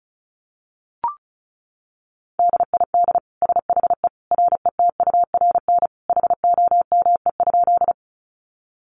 Morse code, heard as a keyed tone, about 34 wpm, 700 Hz. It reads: DID SHE RETURN HOME?
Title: Sentences from Top 800 Words 35wpm